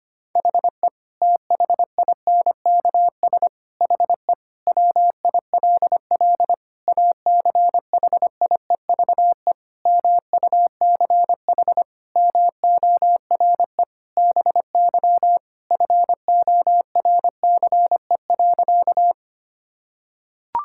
Morse code, heard as a keyed tone, about 25 wpm, 700 Hz. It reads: HE THINKS HE WILL ACHIEVE MUCH MORE BY FORCE.
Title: Sentences from Top 1000 Words - Review Part 6 - 25wpm